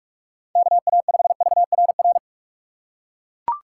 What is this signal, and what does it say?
Morse code, heard as a keyed tone, about 42 wpm, 700 Hz. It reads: KA5VRR
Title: USA Call Signs - 6 Characters - 3x Character Spacing - 45wpm